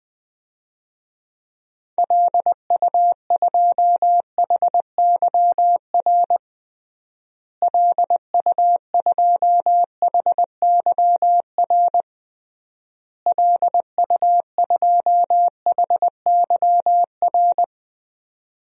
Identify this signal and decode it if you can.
Morse code, heard as a keyed tone, about 20 wpm, 700 Hz. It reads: LU2HYR LU2HYR LU2HYR
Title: DXCC-100 Call Signs - 6 Characters - Repeated 3x - 20wpm